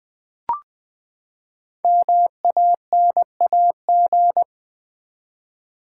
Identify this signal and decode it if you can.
Morse code, heard as a keyed tone, about 20 wpm, 700 Hz. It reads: MANAG
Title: Build A Word - 5 Letters At A Time - 10-Letter Words - 20wpm